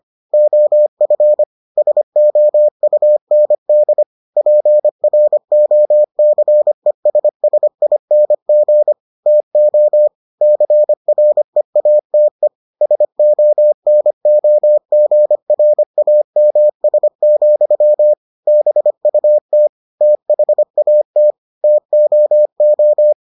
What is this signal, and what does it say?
Morse code, heard as a keyed tone, about 25 wpm, 600 Hz. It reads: OF SOUND PROCESSING TO CREATE SONOGRAMS, BUT THAT TOO